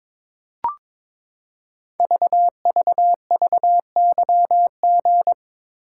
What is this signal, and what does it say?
Morse code, heard as a keyed tone, about 22 wpm, 700 Hz. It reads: VVVYG
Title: Code Groups - 5 Letters - 22wpm